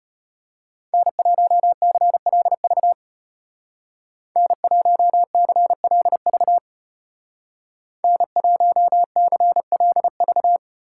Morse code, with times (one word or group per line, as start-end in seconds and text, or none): 0.9-2.9 N1CLV
4.4-6.6 N1CLV
8.0-10.6 N1CLV